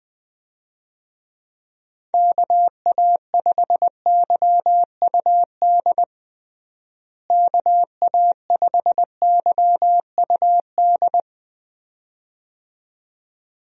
Morse code, heard as a keyed tone, about 20 wpm, 700 Hz. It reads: KA5YUD KA5YUD
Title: USA Call Signs - 6 Characters - Repeated 2x - 20wpm